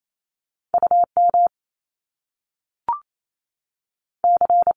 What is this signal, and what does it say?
Morse code, heard as a keyed tone, about 28 wpm, 700 Hz. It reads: U M C
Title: Top Two Letter Combinations in Words 28wpm